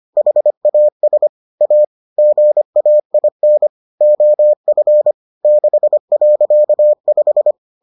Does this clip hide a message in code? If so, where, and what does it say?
Yes: Morse code from 0.2 to 7.5 s, heard as a keyed tone, about 25 wpm, 600 Hz. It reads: HAS A GAIN OF 6.5